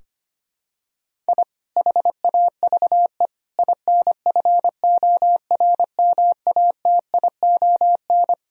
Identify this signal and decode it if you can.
Morse code, heard as a keyed tone, about 25 wpm, 700 Hz. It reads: I HAVE INFORMATION
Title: Sentences from Top 1000 Words - Review Part 3 - 25wpm